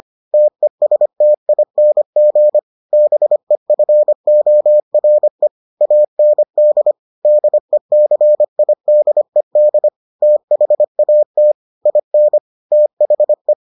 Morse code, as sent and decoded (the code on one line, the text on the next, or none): - . ... - .. -. --. / -... ..-. --- .-. / .- -. -.. / -.. -.-. .. -.. -.. / - .... .- - / .. -. / - .... .
TESTING BEFORE AND DECIDED THAT IN THE